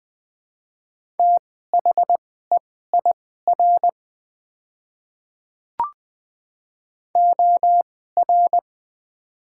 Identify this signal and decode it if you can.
Morse code, heard as a keyed tone, about 20 wpm, spaced as for 15 wpm, 700 Hz. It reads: THEIR OR
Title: 2x Character Spacing - Top 200 Words - 20wpm